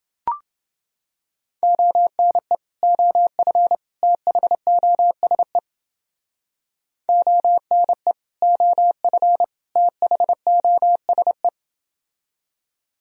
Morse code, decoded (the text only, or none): ONE OF THOSE ONE OF THOSE